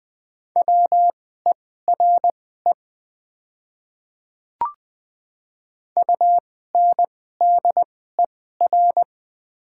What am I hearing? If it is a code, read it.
Morse code, heard as a keyed tone, about 20 wpm, spaced as for 14 wpm, 700 Hz. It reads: WERE UNDER
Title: 2x Character Spacing - Top 200 Words - 20wpm